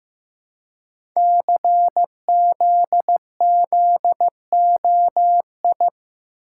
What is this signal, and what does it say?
Morse code, heard as a keyed tone, about 15 wpm, 700 Hz. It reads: CZZOI
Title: Code Groups - 5 Letters - 15wpm